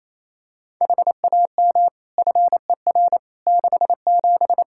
Morse code, heard as a keyed tone, about 27 wpm, 700 Hz. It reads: HAM FER 67
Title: Practice QSO - Exchange 3 - More Info - 28wpm